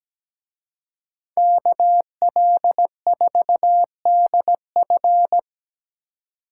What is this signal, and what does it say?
Morse code, heard as a keyed tone, about 20 wpm, 700 Hz. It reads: KL4DF
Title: Speed Racing - USA Call Signs 17wpm